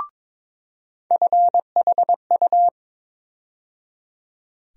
Morse code, heard as a keyed tone, about 22 wpm, 700 Hz. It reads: FHU
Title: USA Call Sign Postfixes - 3 Letters 22wpm